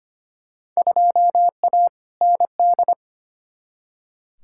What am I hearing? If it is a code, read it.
Morse code, heard as a keyed tone, about 25 wpm, 700 Hz. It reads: 2A ND
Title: Field Day Exchanges Encore - 25wpm